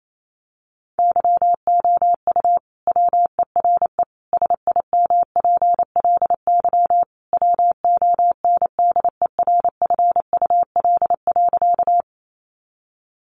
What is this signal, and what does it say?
Morse code, heard as a keyed tone, about 28 wpm, 700 Hz. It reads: YOU WERE SIMPLY WONDERFUL.
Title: Sentences from Top 1100 Words 28wpm